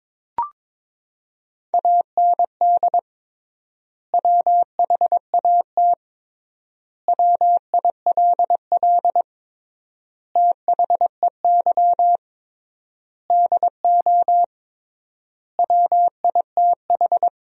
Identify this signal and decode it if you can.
Morse code, heard as a keyed tone, about 22 wpm, 700 Hz. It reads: AND WHAT WILL THEY DO WITH
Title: 3x Word-Spacing - Sentences from Top 200 Words - 22wpm